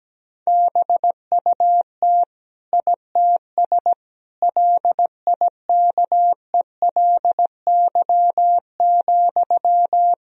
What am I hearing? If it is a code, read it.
Morse code, heard as a keyed tone, about 17 wpm, 700 Hz. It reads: BUT ITS LIKELY,
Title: Sentences from Top 1000 Words - Review Part 4 - 17wpm